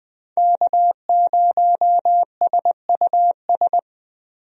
Morse code, heard as a keyed tone, about 20 wpm, 700 Hz. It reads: K0SUS
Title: USA Call Signs - 5 Characters - Repeated 2x - 20wpm